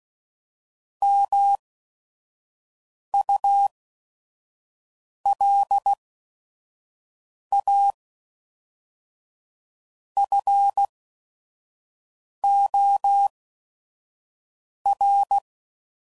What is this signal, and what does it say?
Morse code, heard as a keyed tone, about 16 wpm, 800 Hz. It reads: M U L A F O R